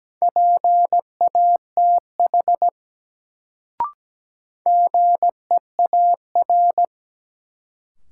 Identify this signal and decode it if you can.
Morse code, heard as a keyed tone, about 17 wpm, 700 Hz. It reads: PATH GEAR
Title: Sentences from Top 1400 Words - New Word Drill - 17wpm